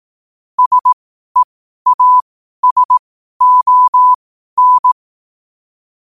Morse code, heard as a keyed tone, about 18 wpm, spaced as for 12 wpm, 1000 Hz. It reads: SEASON